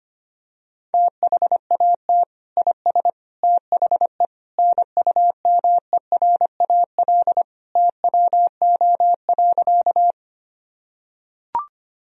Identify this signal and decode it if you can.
Morse code, heard as a keyed tone, about 25 wpm, 700 Hz. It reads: THAT IS THE NUMERAL TWO.
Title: Sentences from Top 1000 Words - Review Part 2 - 25wpm